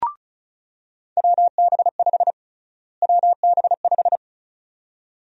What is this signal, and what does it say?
Morse code, heard as a keyed tone, about 36 wpm, 700 Hz. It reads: WB5 WB5
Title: USA Call Sign Prefixes - Repeated 2x - 35wpm